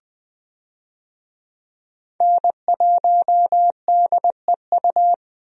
Morse code, heard as a keyed tone, about 20 wpm, 700 Hz. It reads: N1DEU
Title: USA Call Signs - 5 Characters - Repeated 2x - 20wpm